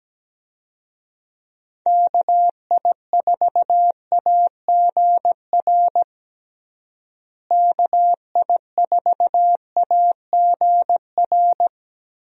Morse code, heard as a keyed tone, about 17 wpm, 700 Hz. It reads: KI4AGR KI4AGR
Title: USA Call Signs - 6 Characters - Repeated 2x - 17wpm